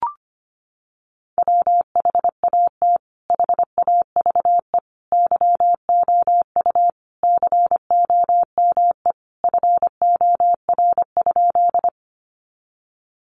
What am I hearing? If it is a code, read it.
Morse code, heard as a keyed tone, about 25 wpm, 700 Hz. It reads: WHAT HAVE YOU COME FOR?
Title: Sentences from Top 100 Words 25wpm